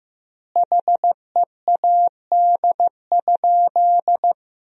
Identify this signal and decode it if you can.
Morse code, heard as a keyed tone, about 15 wpm, 700 Hz. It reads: HEAD?
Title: 4x Word-Spacing - Sentences from Top 300 Words - 15wpm